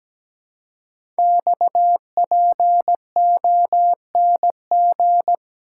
Morse code, heard as a keyed tone, about 17 wpm, 700 Hz. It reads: XPONG
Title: Code Groups - 5 Letters - 17wpm